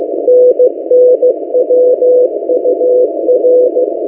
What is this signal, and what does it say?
Morse code, heard as a keyed tone, about 16 wpm, 500 Hz. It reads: NNWUR
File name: M08a- -1355utc-8096khz